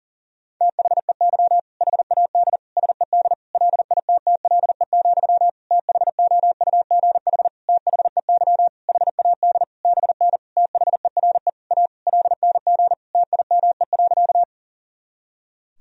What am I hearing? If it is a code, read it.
Morse code, heard as a keyed tone, about 37 wpm, 700 Hz. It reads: THEY HAD SED LITTLE, THOUGH THEY HAD BN THERE A LNG TIME.
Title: Sentences from Top 400 Words and Abbreviations 40wpm